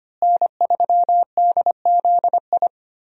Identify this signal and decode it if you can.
Morse code, heard as a keyed tone, about 25 wpm, 700 Hz. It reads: N3DZI
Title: USA Call Signs - 5 Characters - Repeated 2x - 25wpm